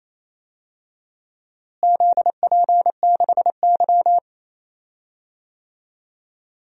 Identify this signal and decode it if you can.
Morse code, heard as a keyed tone, about 28 wpm, 700 Hz. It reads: ZP6Y
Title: DXCC-100 Call Signs - 4 Characters - Repeated 2x - 28wpm